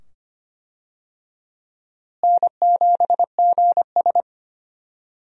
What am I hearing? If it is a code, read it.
Morse code, heard as a keyed tone, about 25 wpm, 700 Hz. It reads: N7GS
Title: USA Call Signs - 4 Characters - Repeated 2x - 25wpm